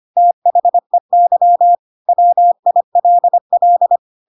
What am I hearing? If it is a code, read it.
Morse code, heard as a keyed tone, about 25 wpm, 700 Hz. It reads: THEY WILL
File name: Pinocchio / Pinocchio ch4 25 25 ik1hge